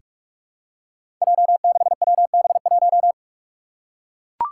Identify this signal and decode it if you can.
Morse code, heard as a keyed tone, about 41 wpm, 700 Hz. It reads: J6WB1
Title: Five Challenging Letter-Number Combinations 45wpm